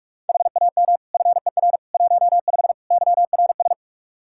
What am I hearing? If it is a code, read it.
Morse code, heard as a keyed tone, about 46 wpm, 700 Hz. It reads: HAM FER 15 YRS
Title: Practice QSO - Exchange 3 - More Info - 45wpm